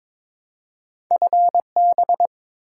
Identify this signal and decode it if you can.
Morse code, heard as a keyed tone, about 22 wpm, 690 Hz. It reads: FB
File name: Top 20 QSO Elements - 2x Character Spacing - 22wpm